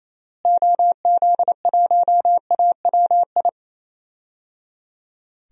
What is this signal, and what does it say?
Morse code, heard as a keyed tone, about 28 wpm, 700 Hz. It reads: OZ1AWI
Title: DXCC-100 Call Signs - 6 Characters - Repeated 3x - 28wpm